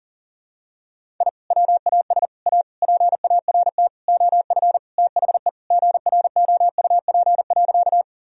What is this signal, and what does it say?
Morse code, heard as a keyed tone, about 40 wpm, 700 Hz. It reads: I WAS A PART OF THE GROUP.
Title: Sentences from Top 1000 Words - Review Part 1 - 40wpm